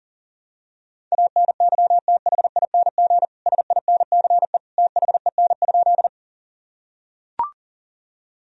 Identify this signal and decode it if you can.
Morse code, heard as a keyed tone, about 40 wpm, 700 Hz. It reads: ANYTHING SINCE THEN?
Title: Sentences from Top 1000 Words - Review Part 1 - 40wpm